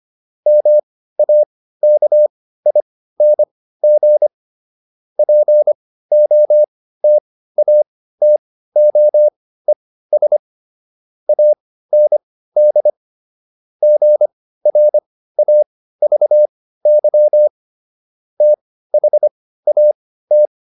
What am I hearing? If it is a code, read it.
Morse code, heard as a keyed tone, about 25 wpm, spaced as for 15 wpm, 600 Hz. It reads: MAKING POTATOES AND GRAVY THAT